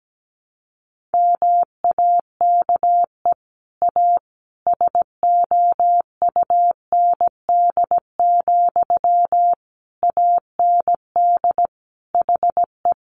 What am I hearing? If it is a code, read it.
Morse code, heard as a keyed tone, about 17 wpm, 700 Hz. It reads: MAKE A SOUND, AND HE
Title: Sentences from Top 1400 Words 17wpm